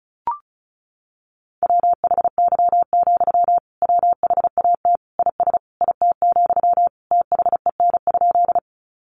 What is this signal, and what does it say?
Morse code, heard as a keyed tone, about 34 wpm, 700 Hz. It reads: WHY, WHAT IS IT, THEN?
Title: Sentences from Top 200 Words 35wpm